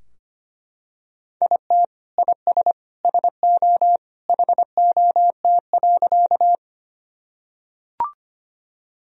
Morse code, heard as a keyed tone, about 25 wpm, 700 Hz. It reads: IT IS SO HOT.
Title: Sentences from Top 1000 Words - Review Part 3 - 25wpm